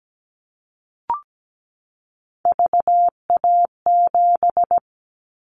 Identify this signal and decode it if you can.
Morse code, heard as a keyed tone, about 17 wpm, 700 Hz. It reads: VA7
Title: Callsign Prefixes for Top 50 DXCC - 17wpm